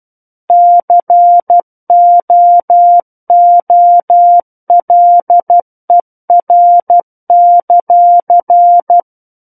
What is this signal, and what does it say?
Morse code, heard as a keyed tone, about 12 wpm, 700 Hz. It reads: COOLER;
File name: Book One / 03-Chapter 3 - 12 WPM